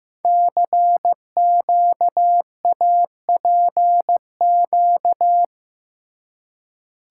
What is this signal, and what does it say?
Morse code, heard as a keyed tone, about 15 wpm, 700 Hz. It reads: CQAPQ
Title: Code Groups - 5 Letters - 15wpm